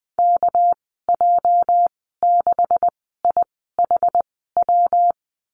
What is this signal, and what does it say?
Morse code, heard as a keyed tone, about 20 wpm, spaced as for 15 wpm, 700 Hz. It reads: KJ6IHW
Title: 238 Lesson 38g - Letter q - Call Signs 20wpm